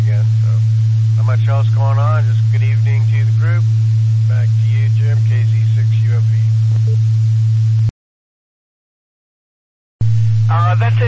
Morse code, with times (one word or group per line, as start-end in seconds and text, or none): none